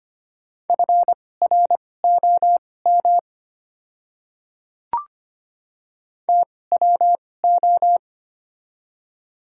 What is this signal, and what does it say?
Morse code, heard as a keyed tone, about 25 wpm, spaced as for 18 wpm, 700 Hz. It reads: FROM TWO
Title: 2x Character Spacing - Top 100 Words - 25wpm